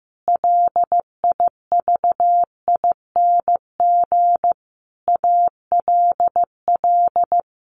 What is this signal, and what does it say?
Morse code, heard as a keyed tone, about 15 wpm, 700 Hz. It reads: LIVING ALL
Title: Sentences from Top 1500 Words 15wpm